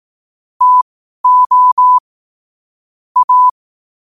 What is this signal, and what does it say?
Morse code, heard as a keyed tone, about 18 wpm, spaced as for 12 wpm, 1000 Hz. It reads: TO A
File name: qotdfiles / 12wpm